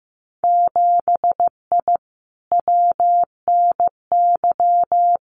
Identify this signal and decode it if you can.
Morse code, heard as a keyed tone, about 15 wpm, 700 Hz. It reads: 7I WNY
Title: Winter Field Day Exchanges Encore 15wpm